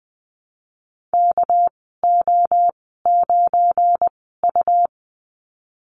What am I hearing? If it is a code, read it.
Morse code, heard as a keyed tone, about 19 wpm, spaced as for 14 wpm, 700 Hz. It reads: KO9U
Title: 200 Lesson 32g - Letter k - Call Signs 20wpm